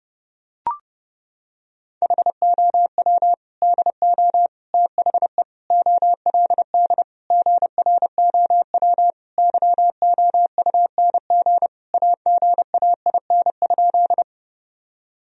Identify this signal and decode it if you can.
Morse code, heard as a keyed tone, about 30 wpm, 700 Hz. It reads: HOW DO THE OLD GROW YOUNG AGAIN?
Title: Sentences from Top 1000 Words - Review Part 1 - 30wpm